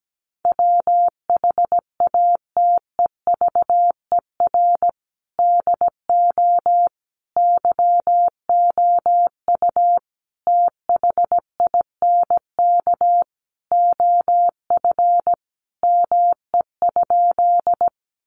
Morse code, heard as a keyed tone, about 17 wpm, 700 Hz. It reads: WHATEVER DO YOU THINK OF ME?
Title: Sentences from Top 1100 Words 17wpm